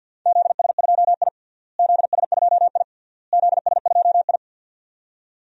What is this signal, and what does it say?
Morse code, heard as a keyed tone, about 50 wpm, 700 Hz. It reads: ZS2I ZS2I ZS2I
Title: DXCC-100 Call Signs - 4 Characters - Repeated 3x - 50wpm